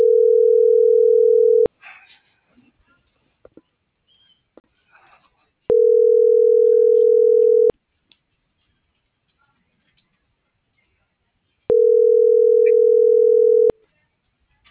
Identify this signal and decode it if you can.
Ringback tone (North America), 440 + 480 Hz.